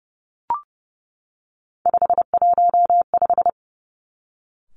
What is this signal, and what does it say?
Morse code, heard as a keyed tone, about 31 wpm, 700 Hz. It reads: 515
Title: Three Challenging Letter-Number Combinations 30wpm